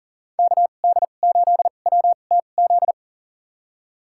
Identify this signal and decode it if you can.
Morse code, heard as a keyed tone, about 39 wpm, spaced as for 28 wpm, 700 Hz. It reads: KD8WTZ
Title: USA Call Signs - 6 Characters - 2x Character Spacing - 40wpm